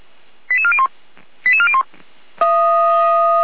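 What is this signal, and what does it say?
Five-tone selective call (ZVEI), heard as quick runs of five tones, tones of about 70 ms. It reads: 88401 88401
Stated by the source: ZVEI five-tone call received on an SDR.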